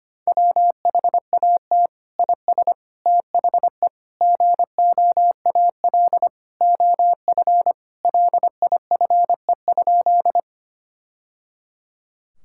Morse code, heard as a keyed tone, about 25 wpm, 700 Hz. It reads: WHAT IS THE GOAL OF LIFE?
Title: Sentences from Top 1000 Words - Review Part 5 - 25wpm